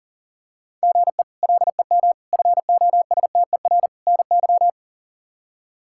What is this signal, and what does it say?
Morse code, heard as a keyed tone, about 40 wpm, 700 Hz. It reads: GE LEM FOSTER NY
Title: K1USN SST QSO Practice - 40wpm